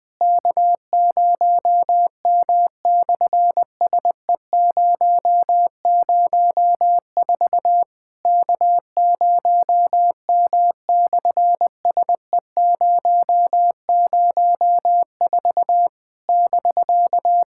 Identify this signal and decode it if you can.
Morse code, heard as a keyed tone, about 20 wpm, 700 Hz. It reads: K0M/SE004 K0M/SE004 <BK>
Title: SOTA QSO Practice 20wpm